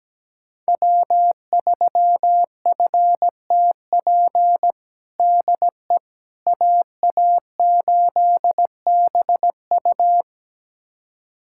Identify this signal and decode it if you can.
Morse code, heard as a keyed tone, about 17 wpm, 700 Hz. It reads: W3FTP DE AA8BU
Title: USA Callsign DE Callsign - 5 Characters - 17wpm